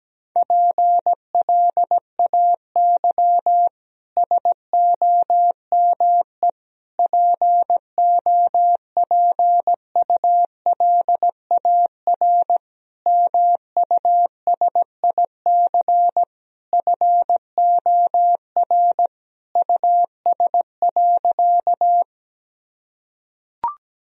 Morse code, heard as a keyed tone, about 17 wpm, 700 Hz. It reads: PLAY SOME POPULAR MUSIC FOR US.
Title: Sentences from Top 1000 Words - Review Part 3 - 17wpm